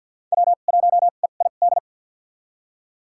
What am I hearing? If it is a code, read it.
Morse code, heard as a keyed tone, about 46 wpm, spaced as for 35 wpm, 700 Hz. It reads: W1EID
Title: USA Call Signs - 2x Character Spacing - 50wpm